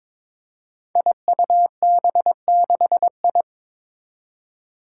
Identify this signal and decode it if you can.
Morse code, heard as a keyed tone, about 22 wpm, 700 Hz. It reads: IUB6I
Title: Five Challenging Letter-Number Combinations 22wpm